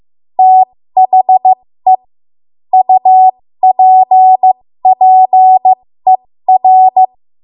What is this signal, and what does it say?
Morse code, heard as a keyed tone, about 15 wpm, spaced as for 13 wpm, 750 Hz. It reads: THE UPPER
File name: Morse / 13WPM